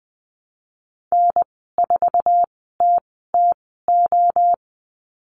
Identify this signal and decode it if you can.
Morse code, heard as a keyed tone, about 20 wpm, spaced as for 15 wpm, 700 Hz. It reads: N4TTO
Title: 048 Lesson 9g - Number 4 - Call Signs 20wpm